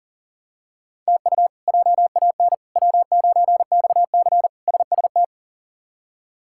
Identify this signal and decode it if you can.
Morse code, heard as a keyed tone, about 40 wpm, 700 Hz. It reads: TU JAN W9XC SST
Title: K1USN SST QSO Practice - 40wpm